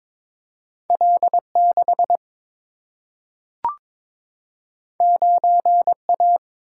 Morse code, heard as a keyed tone, about 22 wpm, 700 Hz. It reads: L6 9A
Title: Single-Letter Single-Number 22wpm